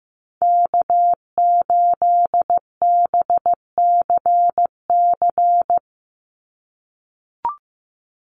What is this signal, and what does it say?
Morse code, heard as a keyed tone, about 15 wpm, 700 Hz. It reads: K8BCC
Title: USA Call Signs 15wpm